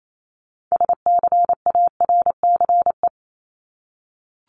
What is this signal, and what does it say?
Morse code, heard as a keyed tone, about 28 wpm, 700 Hz. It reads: SCARCE
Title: LY Suffix - Top 100 Words with Base Words 28wpm